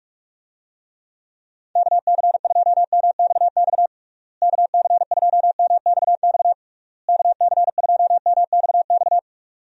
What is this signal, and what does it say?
Morse code, heard as a keyed tone, about 46 wpm, 700 Hz. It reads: KC2MXX KC2MXX KC2MXX
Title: USA Call Signs - 6 Characters - Repeated 3x - 45wpm